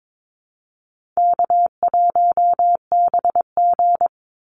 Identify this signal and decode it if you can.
Morse code, heard as a keyed tone, about 22 wpm, 700 Hz. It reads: K1BG
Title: Call Signs - 22wpm